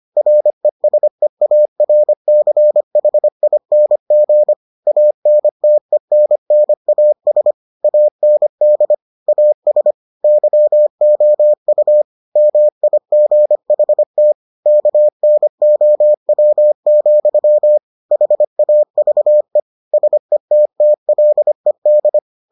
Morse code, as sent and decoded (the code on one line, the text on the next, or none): .-. ... .- .-. -.-. .... .. -. --. / .- -. - . -. -. .- ... / .- -. -.. / .- ... / -.-- --- ..- / -- .. --. .... - / -.- -. --- .-- --..-- / .... .- ...- . / ... - - .-.. -..
RESEARCHING ANTENNAS AND AS YOU MIGHT KNOW, HAVE SETTLED